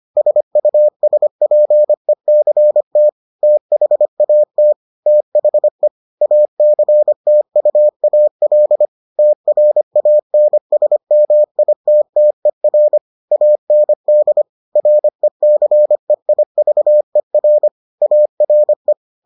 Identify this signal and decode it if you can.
Morse code, heard as a keyed tone, about 25 wpm, 600 Hz. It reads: SUSPECT THAT THE ACTUAL TRANSMITTER AND RECEIVER ARE